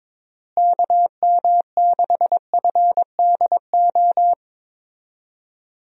Morse code, heard as a keyed tone, about 22 wpm, 700 Hz. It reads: KM6FDO
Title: USA Call Signs - 6 Characters - Repeated 2x - 22wpm